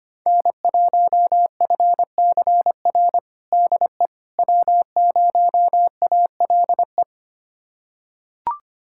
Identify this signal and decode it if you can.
Morse code, heard as a keyed tone, about 25 wpm, 700 Hz. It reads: N1FCR DE W0ALE
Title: USA Callsign DE Callsign - 5 Characters - 25wpm